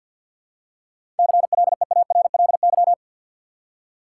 Morse code, heard as a keyed tone, about 53 wpm, 700 Hz. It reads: CLEARLY
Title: Seven Letter Words - 50wpm